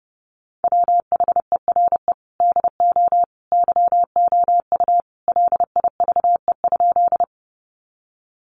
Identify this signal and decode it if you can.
Morse code, heard as a keyed tone, about 30 wpm, 700 Hz. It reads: WHERE DO YOU LIVE?
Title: Sentences from Top 200 Words 30wpm